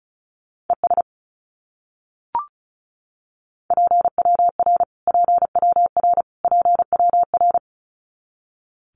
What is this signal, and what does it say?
Morse code, heard as a keyed tone, about 34 wpm, 700 Hz. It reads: ES PWR PWR PWR
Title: Abbreviations Repeated 3x - 35wpm